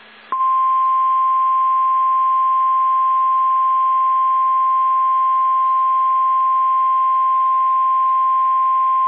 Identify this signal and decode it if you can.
Emergency Alert System (SAME) tones (the 1050 Hz attention tone). It is heard from 0.3 to 8.1 s.
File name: eas / KSDZKDJL / KSDZKDJL